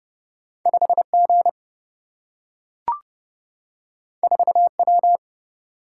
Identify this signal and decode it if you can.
Morse code, heard as a keyed tone, about 30 wpm, 700 Hz. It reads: 5 G 4 W
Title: Single-Letter Single-Number 30wpm